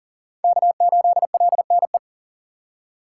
Morse code, heard as a keyed tone, about 40 wpm, 700 Hz. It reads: K8LNE
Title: USA Call Signs - 3x Character Spacing - 40wpm